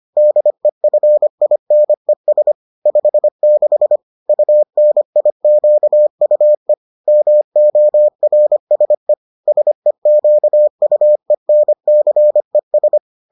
Morse code, heard as a keyed tone, about 25 wpm, 600 Hz. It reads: DEFINES 56 UNIQUE MORSE SEQUENCES